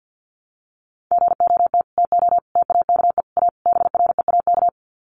Morse code, heard as a keyed tone, about 50 wpm, 700 Hz. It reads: GOT TO TAKE A BREAK
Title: Sets of 5 Words Encore 50wpm